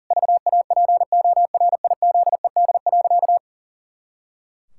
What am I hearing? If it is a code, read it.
Morse code, heard as a keyed tone, about 40 wpm, 700 Hz. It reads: VAPORIZED.
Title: Pangram Challenge v2 - 40wpm